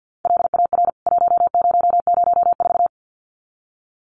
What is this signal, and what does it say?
Morse code, heard as a keyed tone, about 49 wpm, 700 Hz. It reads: LAR 1004
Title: CWT Exchange 50wpm